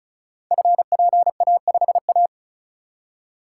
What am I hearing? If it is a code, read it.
Morse code, heard as a keyed tone, about 36 wpm, 700 Hz. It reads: FPA5A
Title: Code Groups - 5 Letter:Numbers - 35wpm